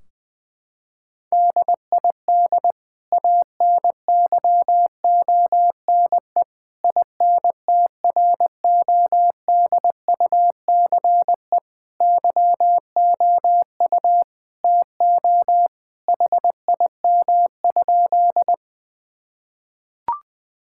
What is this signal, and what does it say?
Morse code, heard as a keyed tone, about 20 wpm, 700 Hz. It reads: DID ANYONE INTRODUCE YOU TO HIM?
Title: Sentences from Top 1600 Words 20wpm